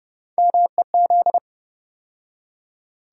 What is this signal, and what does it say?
Morse code, heard as a keyed tone, about 30 wpm, 700 Hz. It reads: MEZ
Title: USA Call Sign Postfixes - 3 Letters 30wpm